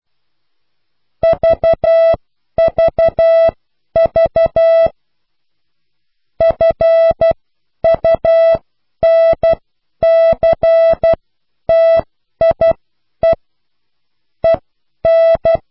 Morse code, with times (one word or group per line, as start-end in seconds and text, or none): 1.2-4.9 VVV
6.4-13.3 FUNCTIE
14.4-15.6 EN